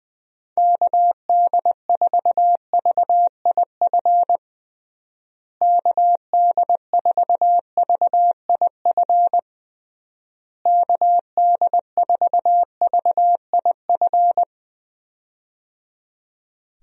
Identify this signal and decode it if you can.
Morse code, heard as a keyed tone, about 20 wpm, 700 Hz. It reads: KD4VIF KD4VIF KD4VIF
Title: USA Call Signs - 6 Characters - Repeated 3x - 20wpm